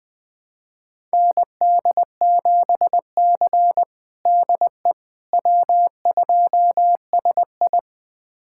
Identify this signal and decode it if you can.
Morse code, heard as a keyed tone, about 20 wpm, 700 Hz. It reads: ND7C DE W2SI
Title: USA Callsign DE Callsign - 4 Characters - 20wpm